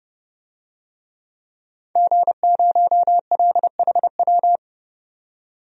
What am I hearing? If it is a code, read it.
Morse code, heard as a keyed tone, about 31 wpm, 700 Hz. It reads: G0LHW
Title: DXCC-100 Call Signs - 5 Characters - 30wpm